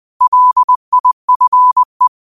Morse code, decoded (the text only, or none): LIFE